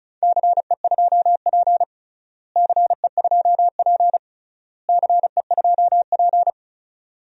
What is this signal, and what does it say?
Morse code, heard as a keyed tone, about 34 wpm, 700 Hz. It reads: CE2P CE2P CE2P
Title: DXCC-100 Call Signs - 4 Characters - Repeated 3x - 35wpm